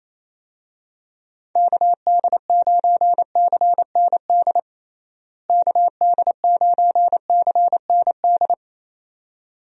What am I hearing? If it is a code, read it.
Morse code, heard as a keyed tone, about 28 wpm, 700 Hz. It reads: KD9CND KD9CND
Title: USA Call Signs - 6 Characters - Repeated 2x - 28wpm